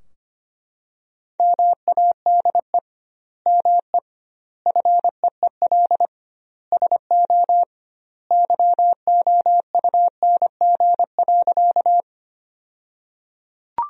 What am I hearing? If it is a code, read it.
Morse code, heard as a keyed tone, about 25 wpm, 700 Hz. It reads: MADE ME FEEL SO YOUNG.
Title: 2x Word-Spacing - Sentences from Top 400 Words - 25wpm